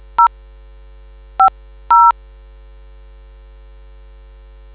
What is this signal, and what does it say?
DTMF, heard as telephone touch tones, tones of about 110 ms. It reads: 0 50